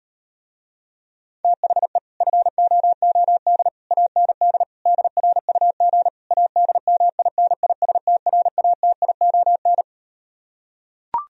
Morse code, heard as a keyed tone, about 38 wpm, 700 Hz. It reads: THE FOOD AND DRUG ADMINISTRATION